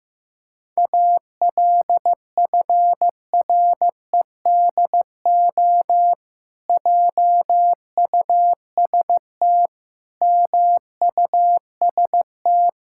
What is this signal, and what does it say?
Morse code, heard as a keyed tone, about 15 wpm, 700 Hz. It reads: ALFREDO JUST MUST
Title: Pangram Challenge v2 - 15wpm